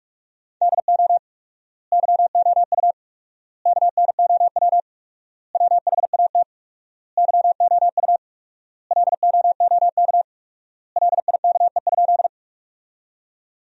Morse code, heard as a keyed tone, about 45 wpm, 700 Hz. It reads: DO YOU KNOW WHAT YOU LOOK LIKE?
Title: 4x Word-Spacing - Sentences from Top 100 Words - 45wpm